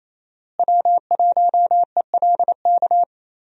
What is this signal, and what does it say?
Morse code, heard as a keyed tone, about 28 wpm, 700 Hz. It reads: W1ELK
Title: USA Call Signs - 5 Characters - Repeated 3x - 28wpm